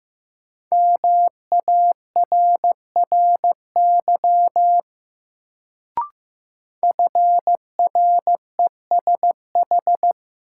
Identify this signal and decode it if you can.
Morse code, heard as a keyed tone, about 15 wpm, 700 Hz. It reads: MARRY FRESH
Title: Sentences from Top 1200 Words - New Word Drill Alt - 15wpm